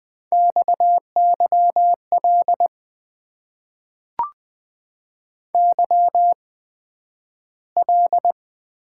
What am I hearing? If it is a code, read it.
Morse code, heard as a keyed tone, about 20 wpm, spaced as for 11 wpm, 700 Hz. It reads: XYL YL
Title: Top 90 QSO Elements - 8x Character Spacing - 20wpm